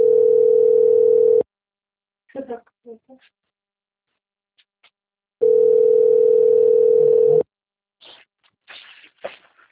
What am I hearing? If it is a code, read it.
Ringback tone (North America), 440 + 480 Hz.